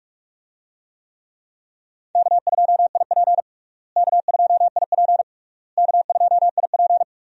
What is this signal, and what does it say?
Morse code, heard as a keyed tone, about 46 wpm, 700 Hz. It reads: K2IP K2IP K2IP
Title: USA Call Signs - 4 Characters - Repeated 3x - 45wpm